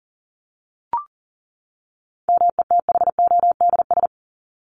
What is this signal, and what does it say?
Morse code, heard as a keyed tone, about 40 wpm, 700 Hz. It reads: METHODS
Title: ES:S Suffix - Top 100 Words with Base Words 40wpm